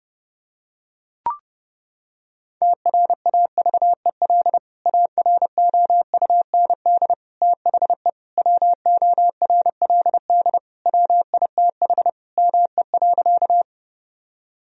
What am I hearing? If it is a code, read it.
Morse code, heard as a keyed tone, about 30 wpm, 700 Hz. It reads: TRAVEL AROUND THE WORLD WITH ME.
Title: Sentences from Top 1000 Words - Review Part 3 - 30wpm